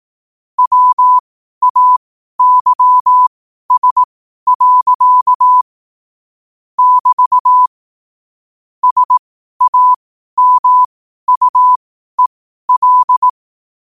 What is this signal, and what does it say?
Morse code, heard as a keyed tone, about 18 wpm, spaced as for 12 wpm, 1000 Hz. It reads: WAYS. <BT> SAMUEL
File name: qotdfiles / 12wpm